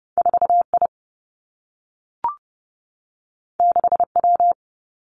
Morse code, heard as a keyed tone, about 30 wpm, 700 Hz. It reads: 4 I 6 W
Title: Two Challenging Letter-Number Combinations 30wpm